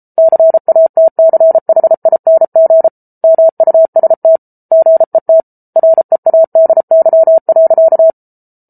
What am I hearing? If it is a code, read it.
Morse code, heard as a keyed tone, about 34 wpm, 650 Hz. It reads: CATCHING MUST GET READY.